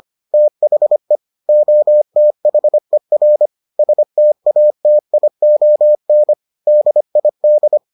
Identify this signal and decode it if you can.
Morse code, heard as a keyed tone, about 25 wpm, 600 Hz. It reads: THE OTHER STATION DID